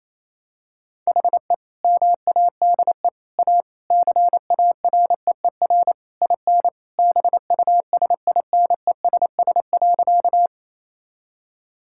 Morse code, heard as a keyed tone, about 28 wpm, 700 Hz. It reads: HE MADE A CAREER IN BUSINESS.
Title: Sentences from Top 1000 Words - Review Part 6 - 28wpm